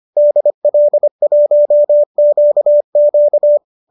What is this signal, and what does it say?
Morse code, heard as a keyed tone, about 25 wpm, 570 Hz. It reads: DL1QQ
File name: qso219 25